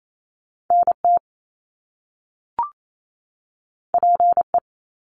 Morse code, heard as a keyed tone, about 28 wpm, 700 Hz. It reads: NT PE
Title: Canadian Provinces and Territories 28wpm